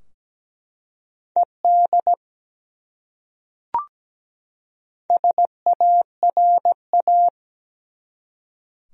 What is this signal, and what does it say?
Morse code, heard as a keyed tone, about 17 wpm, 700 Hz. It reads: ED SARA
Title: USA Common Names - Max 4 Letters - 17wpm